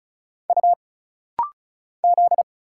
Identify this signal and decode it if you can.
Morse code, heard as a keyed tone, about 36 wpm, 700 Hz. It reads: U Z
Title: Single Letters - Rapid-Fire 35wpm